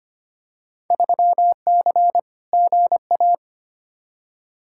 Morse code, heard as a keyed tone, about 25 wpm, 700 Hz. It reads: 3C GA
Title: Field Day Exchanges Encore - 25wpm